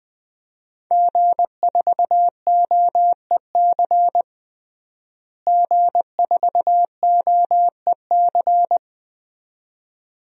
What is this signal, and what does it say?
Morse code, heard as a keyed tone, about 20 wpm, 700 Hz. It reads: G4OEC G4OEC
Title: DXCC-100 Call Signs - 5 Characters - Repeated 2x - 20wpm